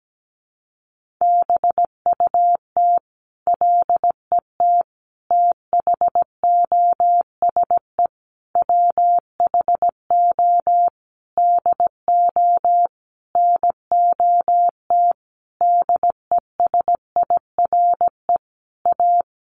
Morse code, heard as a keyed tone, about 17 wpm, 700 Hz. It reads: BUT LET THOSE WHO DO NOT DESIRE A
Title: Sentences from Top 1500 Words 17wpm